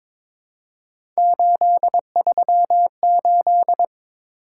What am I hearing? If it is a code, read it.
Morse code, heard as a keyed tone, about 22 wpm, 700 Hz. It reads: 838
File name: practice / 22wpm / Three Letter-Number Combinations - 22wpm